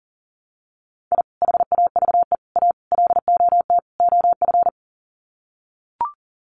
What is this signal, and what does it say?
Morse code, heard as a keyed tone, about 40 wpm, 700 Hz. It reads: I HAVE A LOT OF
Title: Sets of 5 Words 40wpm